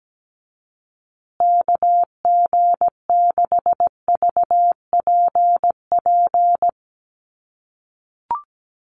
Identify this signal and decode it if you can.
Morse code, heard as a keyed tone, about 17 wpm, 700 Hz. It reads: KG6VPP
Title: USA Call Signs Repeated 3x 17wpm